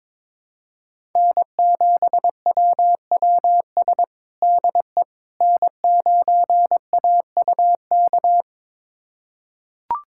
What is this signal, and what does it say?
Morse code, heard as a keyed tone, about 21 wpm, 700 Hz. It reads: N7WWS DE N9AUK
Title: USA Callsign DE Callsign - 5 Characters - 22wpm